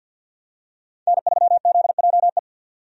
Morse code, heard as a keyed tone, about 51 wpm, 700 Hz. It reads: N3ZJE
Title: USA Call Signs - 5 Characters - 50wpm